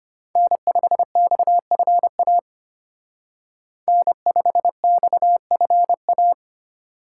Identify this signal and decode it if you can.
Morse code, heard as a keyed tone, about 33 wpm, 700 Hz. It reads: N5XFA N5XFA